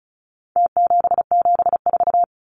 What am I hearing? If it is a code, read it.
Morse code, heard as a keyed tone, about 34 wpm, 700 Hz. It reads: T774
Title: Numbers - Contest Style - 4-Digit 35wpm